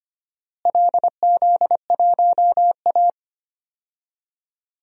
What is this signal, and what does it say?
Morse code, heard as a keyed tone, about 25 wpm, 700 Hz. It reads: LZ1A
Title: DXCC-100 Call Signs - 4 Characters - Repeated 3x - 25wpm